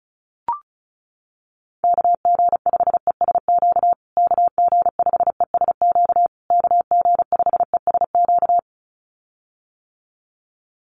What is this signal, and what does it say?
Morse code, heard as a keyed tone, about 34 wpm, 700 Hz. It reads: KG5ESQ KG5ESQ KG5ESQ
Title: USA Call Signs Repeated 3x 35wpm